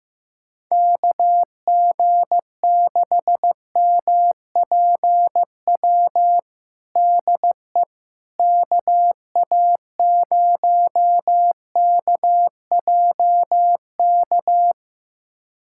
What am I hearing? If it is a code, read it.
Morse code, heard as a keyed tone, about 15 wpm, 700 Hz. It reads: KG6MPW DE KA0KJK
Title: USA Callsign DE Callsign - 6 Characters - 15wpm